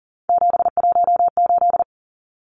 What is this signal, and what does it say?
Morse code, heard as a keyed tone, about 40 wpm, 700 Hz. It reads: 718
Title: Numbers - 3-Digit 40wpm